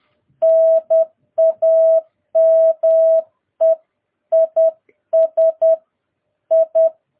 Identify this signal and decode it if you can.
Morse code, heard as a keyed tone, about 10 wpm, 650 Hz. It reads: NAME IS I
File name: jh8cbh / 2022cw